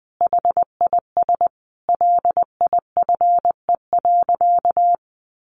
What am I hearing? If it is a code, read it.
Morse code, heard as a keyed tone, about 20 wpm, 700 Hz. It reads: HIS LIFE.
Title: Sentences from Top 1100 Words 20wpm